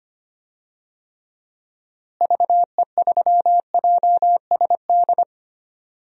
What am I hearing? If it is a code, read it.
Morse code, heard as a keyed tone, about 25 wpm, 700 Hz. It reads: VE3JSD
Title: DXCC-100 Call Signs - 6 Characters - Repeated 3x - 25wpm